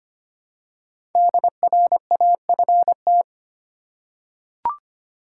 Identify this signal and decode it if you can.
Morse code, heard as a keyed tone, about 25 wpm, 700 Hz. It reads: DRAFT
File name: 5 Letter Words - First 3 Letters Given - 25wpm